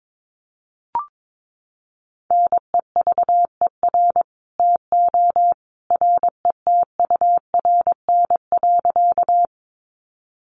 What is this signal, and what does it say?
Morse code, heard as a keyed tone, about 22 wpm, 700 Hz. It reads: NEVER TO RETURN.
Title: Sentences from Top 800 Words 22wpm